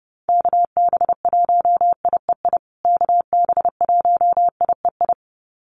Morse code, heard as a keyed tone, about 30 wpm, 700 Hz. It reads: KB1IEI KB1IEI
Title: USA Call Signs Repeated 2x 30wpm